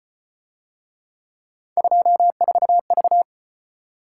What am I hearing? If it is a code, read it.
Morse code, heard as a keyed tone, about 34 wpm, 700 Hz. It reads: 24V